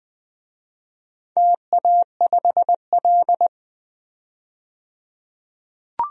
Morse code, heard as a keyed tone, about 20 wpm, 700 Hz. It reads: TA5L
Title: DXCC-100 Call Signs - 4 Characters - 20wpm